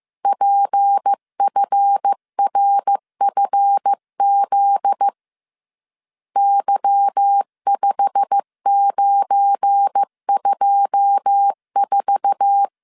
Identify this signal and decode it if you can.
Morse code, heard as a keyed tone, about 15 wpm, 800 Hz. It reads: PFRFZ Y5924
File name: CW / DET / 1 9 1500C S C 14wpm 3245